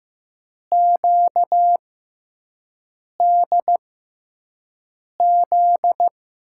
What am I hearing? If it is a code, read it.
Morse code, heard as a keyed tone, about 15 wpm, spaced as for 5 wpm, 700 Hz. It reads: QDZ
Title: USA Call Signs - 6 Characters - 6x Character Spacing - 15wpm